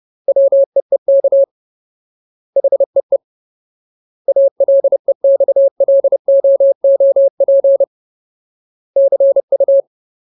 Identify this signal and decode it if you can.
Morse code, heard as a keyed tone, about 30 wpm, 550 Hz. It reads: WEEK HEE ALEXLOOP CU
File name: Ultimate Word List / 30WPM Ultimate